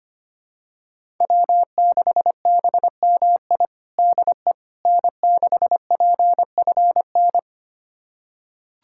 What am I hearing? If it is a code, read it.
Morse code, heard as a keyed tone, about 25 wpm, 700 Hz. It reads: W6BMI DE N6PFN
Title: USA Callsign DE Callsign - 5 Characters - 25wpm